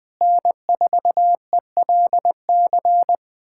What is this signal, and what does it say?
Morse code, heard as a keyed tone, about 20 wpm, 700 Hz. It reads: N4ELC
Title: K1USN SST QSO Practice - 20wpm